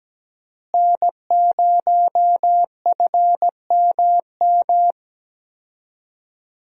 Morse code, heard as a keyed tone, about 17 wpm, 700 Hz. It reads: N0FMM
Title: USA Call Signs - 5 Characters - Repeated 3x - 17wpm